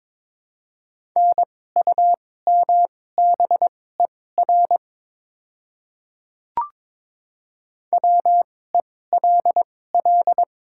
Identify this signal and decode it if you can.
Morse code, heard as a keyed tone, about 22 wpm, spaced as for 16 wpm, 700 Hz. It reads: NUMBER WELL
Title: 2x Character Spacing - Top 500 Words - 22wpm